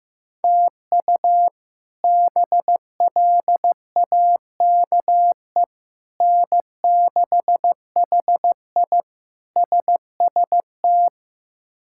Morse code, heard as a keyed tone, about 16 wpm, 700 Hz. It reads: TU BLAKE N6HI SST
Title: K1USN SST QSO Practice - 15wpm